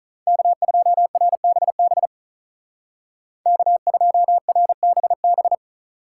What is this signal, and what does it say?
Morse code, heard as a keyed tone, about 44 wpm, 700 Hz. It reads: K2RBB K2RBB